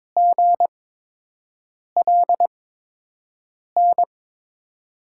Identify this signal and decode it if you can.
Morse code, heard as a keyed tone, about 22 wpm, spaced as for 6 wpm, 700 Hz. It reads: GLN
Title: USA Call Signs - 5 Characters - 8x Character Spacing - 22wpm